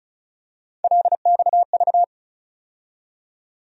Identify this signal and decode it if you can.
Morse code, heard as a keyed tone, about 36 wpm, 700 Hz. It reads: LXV
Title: USA Call Sign Postfixes - 3 Letters 35wpm